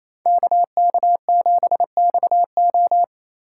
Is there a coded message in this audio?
Yes: Morse code from 0.3 to 3.0 s, heard as a keyed tone, about 37 wpm, 700 Hz. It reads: KK7XO